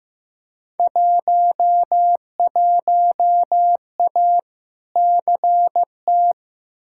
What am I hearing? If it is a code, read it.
Morse code, heard as a keyed tone, about 15 wpm, 700 Hz. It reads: 11A CT
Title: Field Day Exchanges Encore - 15wpm